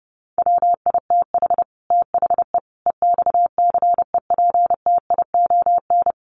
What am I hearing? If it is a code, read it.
Morse code, heard as a keyed tone, about 30 wpm, 700 Hz. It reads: WITH THE EXCEPTION
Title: Sets of 4 Words 30wpm